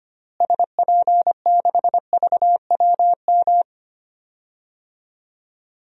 Morse code, heard as a keyed tone, about 25 wpm, 700 Hz. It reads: SP6VWM
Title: DXCC-100 Call Signs - 6 Characters - Repeated 2x - 25wpm